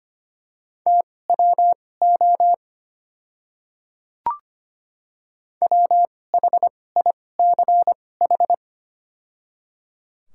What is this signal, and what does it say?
Morse code, heard as a keyed tone, about 25 wpm, spaced as for 18 wpm, 700 Hz. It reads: TWO WHICH
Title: 2x Character Spacing - Top 200 Words - 25wpm